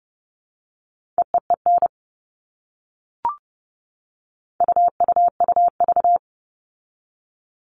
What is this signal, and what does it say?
Morse code, heard as a keyed tone, about 30 wpm, 700 Hz. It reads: EEEN UUUV
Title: All Cut Numbers - 4-Digit 30wpm